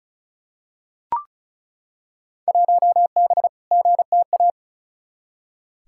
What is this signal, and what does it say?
Morse code, heard as a keyed tone, about 36 wpm, 700 Hz. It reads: 1B GTA
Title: Field Day Exchanges Encore - 35wpm